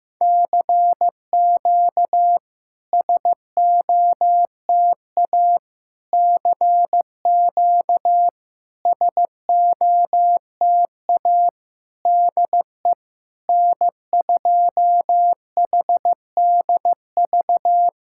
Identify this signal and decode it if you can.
Morse code, heard as a keyed tone, about 15 wpm, 700 Hz. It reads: CQ SOTA CQ SOTA DE N2HDV
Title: SOTA QSO Practice 15wpm